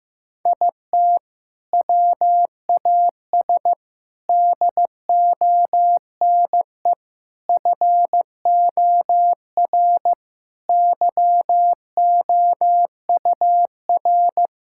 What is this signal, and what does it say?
Morse code, heard as a keyed tone, about 15 wpm, 700 Hz. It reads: IT WAS DONE FOR YOUR
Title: Sentences from Top 1600 Words 15wpm